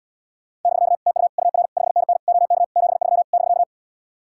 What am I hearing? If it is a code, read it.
Morse code, heard as a keyed tone, about 77 wpm, 700 Hz. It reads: <BK> TU UR 5NN OK 73 <BK>